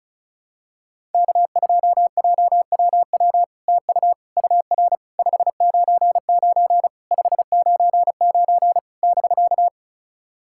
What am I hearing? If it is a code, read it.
Morse code, heard as a keyed tone, about 36 wpm, 700 Hz. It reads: K2JWW TU UR 599 599 <BK>
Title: SOTA QSO Practice 35wpm